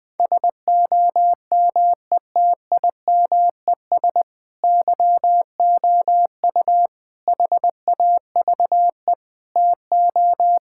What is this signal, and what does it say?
Morse code, heard as a keyed tone, about 21 wpm, 700 Hz. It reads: SOMETIMES YOU HAVE TO
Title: Sentences with 3-Word Phrases - 20wpm